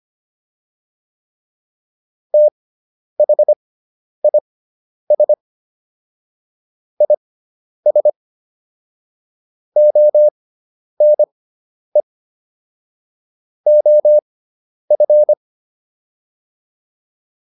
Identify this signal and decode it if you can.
Morse code, heard as a keyed tone, about 25 wpm, spaced as for 10 wpm, 600 Hz. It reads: THIS IS ONE OF
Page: intermediate-practice-files